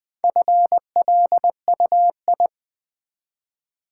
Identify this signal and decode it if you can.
Morse code, heard as a keyed tone, about 20 wpm, 700 Hz. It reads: FLUI
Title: Build A Word - 1 Letter At A Time - 5-Letter Words - 20wpm